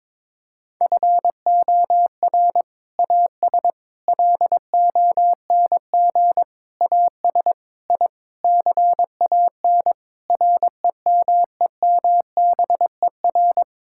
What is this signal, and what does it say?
Morse code, heard as a keyed tone, about 22 wpm, 700 Hz. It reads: FOR AS LONG AS I CAN REMEMBER
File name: practice / 22wpm / Sets of 8 Words - 22wpm